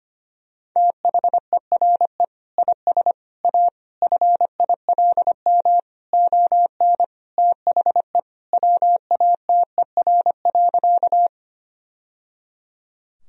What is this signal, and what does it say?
Morse code, heard as a keyed tone, about 25 wpm, 700 Hz. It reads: THERE IS A FILM ON THE WATER.
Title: Sentences from Top 1000 Words - Review Part 3 - 25wpm